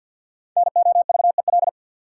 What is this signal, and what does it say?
Morse code, heard as a keyed tone, about 52 wpm, 700 Hz. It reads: NOVEL
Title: Sentences from Top 1300 Words - New Word Drill - 50wpm